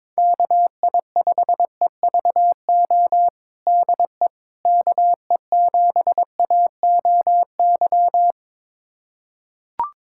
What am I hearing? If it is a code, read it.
Morse code, heard as a keyed tone, about 22 wpm, 700 Hz. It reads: KI5EVO DE KE7AOY
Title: USA Callsign DE Callsign 22wpm